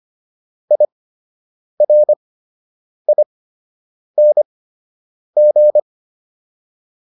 Morse code, heard as a keyed tone, about 25 wpm, spaced as for 8 wpm, 600 Hz. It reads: IRING